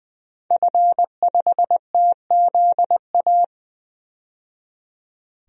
Morse code, heard as a keyed tone, about 20 wpm, 700 Hz. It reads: F5TZA
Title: DXCC-100 Call Signs - 5 Characters - 20wpm